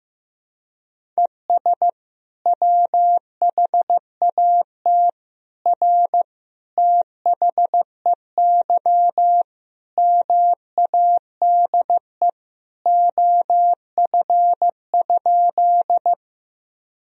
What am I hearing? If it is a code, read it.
Morse code, heard as a keyed tone, about 15 wpm, 700 Hz. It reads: ES WHAT R THEY MADE OF?
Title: Sentences from Top 200 Words and Abbreviations 15wpm